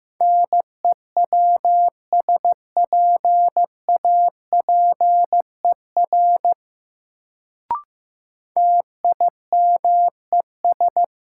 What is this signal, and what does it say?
Morse code, heard as a keyed tone, about 15 wpm, 700 Hz. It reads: NEWSPAPER TIMES
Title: Sentences from Top 1100 Words - New Word Drill Alt - 15wpm